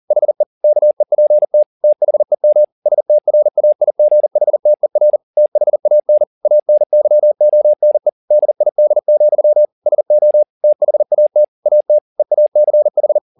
Morse code, heard as a keyed tone, about 40 wpm, 600 Hz. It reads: HE KEPT THEM STRAIGHTER THAN ANYONE DID, SO THAT AT EACH